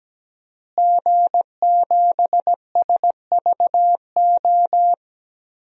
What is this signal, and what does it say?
Morse code, heard as a keyed tone, about 17 wpm, 700 Hz. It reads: G7SVO
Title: DXCC-100 Call Signs - 5 Characters - Repeated 3x - 17wpm